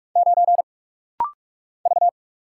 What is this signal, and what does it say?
Morse code, heard as a keyed tone, about 46 wpm, 700 Hz. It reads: 9 V
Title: Single Letter-Number - Rapid-Fire 45wpm